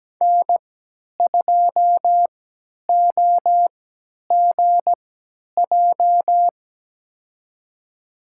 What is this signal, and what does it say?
Morse code, heard as a keyed tone, about 17 wpm, spaced as for 10 wpm, 700 Hz. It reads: N2OGJ
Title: USA Call Signs - 3x Character Spacing - 17wpm